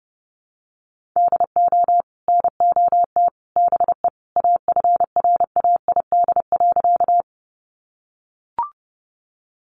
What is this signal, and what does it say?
Morse code, heard as a keyed tone, about 30 wpm, 700 Hz. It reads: DO NOT BE AFRAID.
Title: Sentences from Top 1500 Words 30wpm